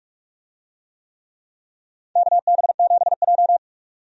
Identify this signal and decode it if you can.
Morse code, heard as a keyed tone, about 46 wpm, 700 Hz. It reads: KB7J
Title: USA Call Signs - 4 Characters - 45wpm